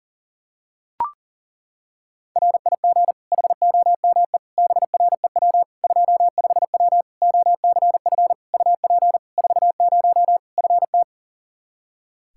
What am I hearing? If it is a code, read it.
Morse code, heard as a keyed tone, about 40 wpm, 700 Hz. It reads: RIG HOME BREW 25W OCF UP 40 FT
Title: Practice QSO - Exchange 2 - Info - 40wpm